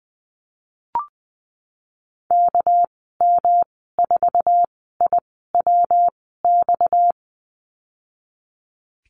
Morse code, heard as a keyed tone, about 20 wpm, spaced as for 15 wpm, 700 Hz. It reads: KM4IWX
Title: 238 Lesson 38g - Letter q - Call Signs 20wpm